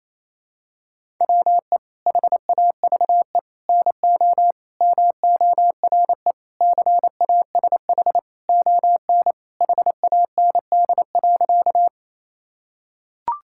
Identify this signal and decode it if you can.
Morse code, heard as a keyed tone, about 28 wpm, 700 Hz. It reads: WE HAVE NO MORE CASH ON HAND.
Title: Sentences from Top 1000 Words - Review Part 5 - 28wpm